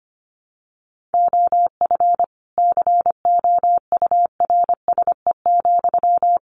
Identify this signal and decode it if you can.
Morse code, heard as a keyed tone, about 25 wpm, 700 Hz. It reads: OF COURSE,
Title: Sentences from Top 1500 Words 25wpm